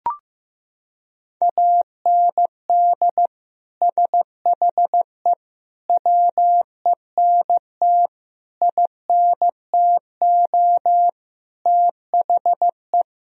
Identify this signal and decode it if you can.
Morse code, heard as a keyed tone, about 16 wpm, 700 Hz. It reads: AND SHE WENT INTO THE
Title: Sentences from Top 1000 Words - Review Part 2 - 15wpm